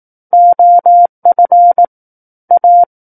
Morse code, heard as a keyed tone, about 18 wpm, 700 Hz. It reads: OF A
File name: Book One / 09-Chapter 9 - 18 WPM